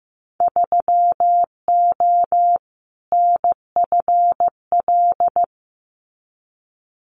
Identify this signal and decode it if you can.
Morse code, heard as a keyed tone, about 15 wpm, 700 Hz. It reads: 3O NFL
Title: Winter Field Day Exchanges 15wpm